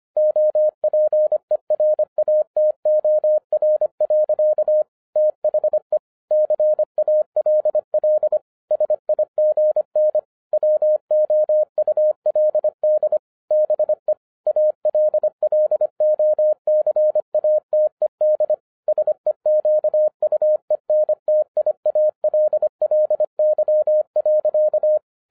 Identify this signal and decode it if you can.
Morse code, heard as a keyed tone, about 26 wpm, 610 Hz. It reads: OPERATOR. THE CALL SIGN WOULD BE ALLOCATED SEQUENTIALLY.